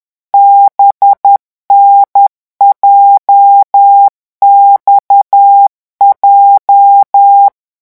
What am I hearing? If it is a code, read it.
Morse code, heard as a keyed tone, about 10 wpm, 800 Hz. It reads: BNJXJ